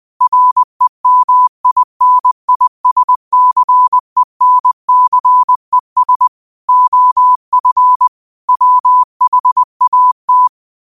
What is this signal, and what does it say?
Morse code, heard as a keyed tone, about 20 wpm, 1000 Hz. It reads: REMINISCENCES OF WHAT